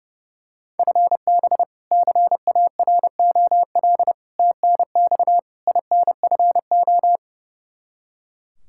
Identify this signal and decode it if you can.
Morse code, heard as a keyed tone, about 32 wpm, 700 Hz. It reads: FB CAROL TNX INFO
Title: K1USN SST QSO Practice 30wpm